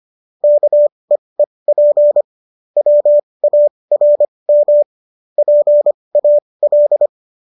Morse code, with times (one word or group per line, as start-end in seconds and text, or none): 0.4-2.2 KEEP
2.8-4.8 WARM
5.4-7.1 PAL